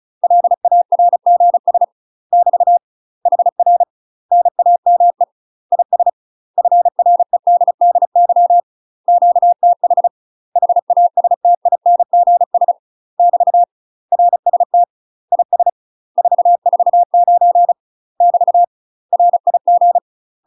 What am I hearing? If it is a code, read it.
Morse code, heard as a keyed tone, about 36 wpm, 700 Hz. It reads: LARGS <BT> HR NAME IS FREDDY QTH HASTINGS <BT> RST IS 449 <BT> RIG
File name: UK MASTER QSOs TRAINING / 35WPM / UK MASTER QSOs 0033